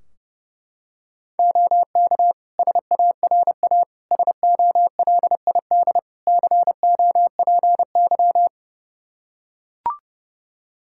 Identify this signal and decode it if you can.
Morse code, heard as a keyed tone, about 30 wpm, 700 Hz. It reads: OK SARA SOLID COPY
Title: Practice QSO - Exchange 3 - More Info - 30wpm